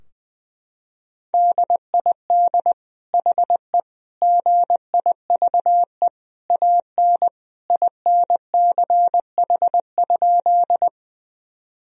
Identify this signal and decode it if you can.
Morse code, heard as a keyed tone, about 20 wpm, 700 Hz. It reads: DID HE GIVE AN INCH?
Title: Sentences from Top 1000 Words - Review Part 3 - 20wpm